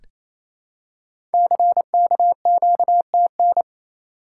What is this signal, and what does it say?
Morse code, heard as a keyed tone, about 28 wpm, 700 Hz. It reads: CKQTN
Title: Code Groups - 5 Letters - 28wpm